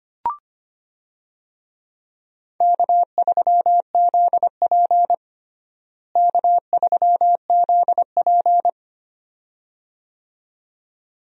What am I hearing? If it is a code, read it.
Morse code, heard as a keyed tone, about 25 wpm, 700 Hz. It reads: K3ZP K3ZP
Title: USA Call Signs - 4 Characters - Repeated 2x - 25wpm